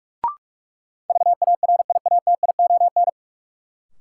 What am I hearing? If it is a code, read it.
Morse code, heard as a keyed tone, about 45 wpm, 700 Hz. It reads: VARIATION
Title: Sentences from Top 1600 Words - New Word Drill - 45wpm